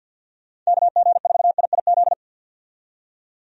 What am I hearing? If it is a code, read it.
Morse code, heard as a keyed tone, about 48 wpm, 700 Hz. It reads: KG4IIZ
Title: USA Call Signs - 6 Characters - 50wpm